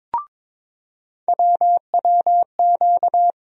Morse code, heard as a keyed tone, about 22 wpm, 700 Hz. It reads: WWQ
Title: USA Call Sign Postfixes - 3 Letters 22wpm